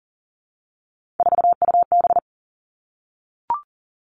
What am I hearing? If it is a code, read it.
Morse code, heard as a keyed tone, about 41 wpm, 700 Hz. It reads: 4UB
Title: Three Challenging Letter-Number Combinations 40wpm